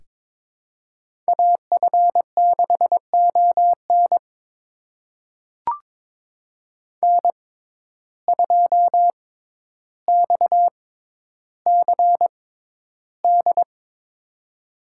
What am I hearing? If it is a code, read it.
Morse code, heard as a keyed tone, about 22 wpm, spaced as for 11 wpm, 700 Hz. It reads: AF6ON N2XCD
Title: USA Call Signs - 5 Characters - 6x Character Spacing - 22wpm